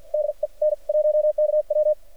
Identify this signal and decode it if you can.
Morse code, heard as a keyed tone, about 49 wpm, spaced as for 41 wpm, 600 Hz, with radio noise. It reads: DE N1KW
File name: archived / CRSnet